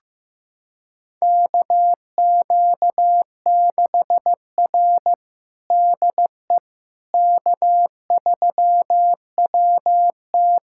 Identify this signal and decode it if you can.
Morse code, heard as a keyed tone, about 15 wpm, 700 Hz. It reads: KQ6R DE K3WT
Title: USA Callsign DE Callsign - 4 Characters - 15wpm